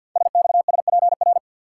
Morse code, heard as a keyed tone, about 50 wpm, 690 Hz. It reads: SXSPR
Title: Code Groups - 5 Letters - 50wpm